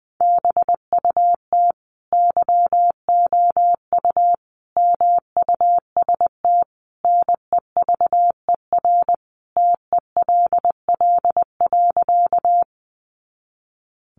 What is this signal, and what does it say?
Morse code, heard as a keyed tone, about 20 wpm, 700 Hz. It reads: BUT YOU MUST NEVER TELL.
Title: Sentences from Top 300 Words 20wpm